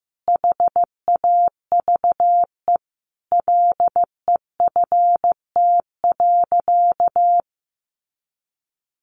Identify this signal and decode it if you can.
Morse code, heard as a keyed tone, about 15 wpm, 700 Hz. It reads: HAVE LEFT.
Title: Sentences from Top 1500 Words 15wpm